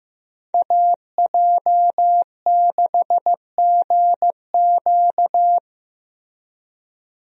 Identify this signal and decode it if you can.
Morse code, heard as a keyed tone, about 15 wpm, 700 Hz. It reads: AJ6GQ
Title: USA Call Signs - 5 Characters - Repeated 3x - 15wpm